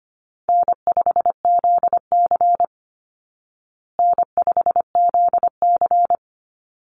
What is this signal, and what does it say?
Morse code, heard as a keyed tone, about 25 wpm, 700 Hz. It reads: N5ZC N5ZC
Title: Easy Call Signs Repeated 3x - 25wpm